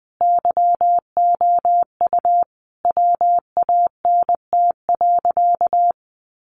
Morse code, heard as a keyed tone, about 20 wpm, 700 Hz. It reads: YOU WANT.
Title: Sentences from Top 1100 Words 20wpm